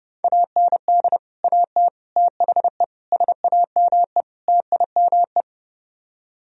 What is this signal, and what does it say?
Morse code, heard as a keyed tone, about 37 wpm, 700 Hz. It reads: AND AT THE SAME TIME
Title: Speed-Racing - Sets of 5 Words 30wpm